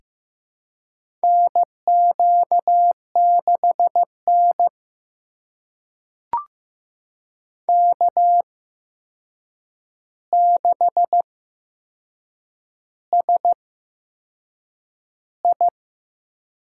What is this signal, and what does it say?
Morse code, heard as a keyed tone, about 15 wpm, spaced as for 6 wpm, 700 Hz. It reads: NQ6N K6SI
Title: USA Call Signs - 8x Character Spacing - 15wpm